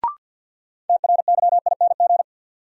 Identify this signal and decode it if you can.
Morse code, heard as a keyed tone, about 53 wpm, 700 Hz. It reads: TRYING
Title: Sets of 2 Words - New Word Drill - 50wpm